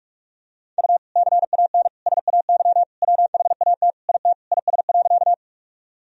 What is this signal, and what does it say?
Morse code, heard as a keyed tone, about 46 wpm, 700 Hz. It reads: U CAN SAY WHAT IT IS.
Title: Sentences from Top 200 Words and Abbreviations 45wpm